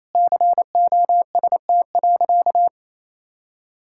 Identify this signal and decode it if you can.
Morse code, heard as a keyed tone, about 27 wpm, 700 Hz. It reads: COST.
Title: Sentences from Top 1000 Words - Review Part 5 - 28wpm